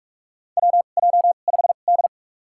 Morse code, heard as a keyed tone, about 44 wpm, spaced as for 32 wpm, 690 Hz. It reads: WJ5D
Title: USA Call Signs - 4 Characters - 2x Character Spacing - 45wpm